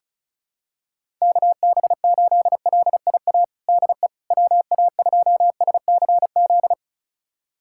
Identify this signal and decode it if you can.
Morse code, heard as a keyed tone, about 34 wpm, 690 Hz. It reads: KB8LIA DE WA2SCZ
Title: USA Callsign DE Callsign - 6 Characters - 35wpm